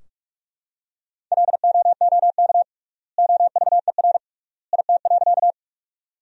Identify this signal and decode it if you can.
Morse code, heard as a keyed tone, about 46 wpm, 690 Hz. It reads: LOOK OVER IT.
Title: 3x Word-Spacing - Sentences from Top 100 Words - 45wpm